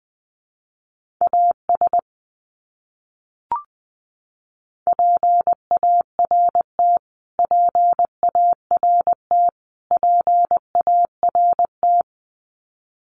Morse code, heard as a keyed tone, about 20 wpm, 700 Hz. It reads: AS PART PART PART
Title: Top 200 Words Repeated 20wpm